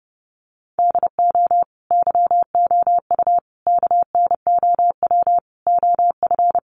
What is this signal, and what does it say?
Morse code, heard as a keyed tone, about 30 wpm, 700 Hz. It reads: DO YOU KNOW OF
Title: Sentences from Top 1200 Words 30wpm